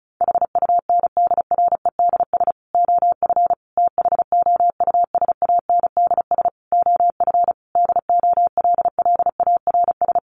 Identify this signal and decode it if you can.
Morse code, heard as a keyed tone, about 34 wpm, 700 Hz. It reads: HUNDREDS OF THOUSANDS OF DOLLARS
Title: Sets of 5 Words Encore 35wpm